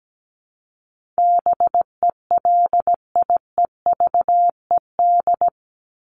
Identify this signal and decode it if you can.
Morse code, heard as a keyed tone, about 17 wpm, 700 Hz. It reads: BELIEVED
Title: ED Suffix - Top 100 Words with Base Words 17wpm